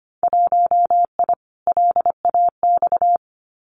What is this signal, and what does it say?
Morse code, heard as a keyed tone, about 25 wpm, 700 Hz. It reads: 1I LAX
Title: Winter Field Day Exchanges Encore 25wpm